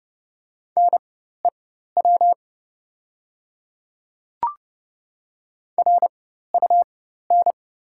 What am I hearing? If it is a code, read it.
Morse code, heard as a keyed tone, about 30 wpm, spaced as for 14 wpm, 700 Hz. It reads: NEW RUN
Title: 4x Character Spacing - Top 300 Words - 30wpm